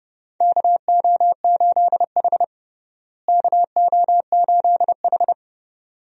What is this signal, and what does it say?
Morse code, heard as a keyed tone, about 29 wpm, 700 Hz. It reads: KO8H KO8H
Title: USA Call Signs - 4 Characters - Repeated 2x - 30wpm